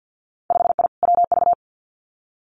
Morse code, heard as a keyed tone, about 50 wpm, 700 Hz. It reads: 5I WV
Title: Winter Field Day Exchanges Encore 50wpm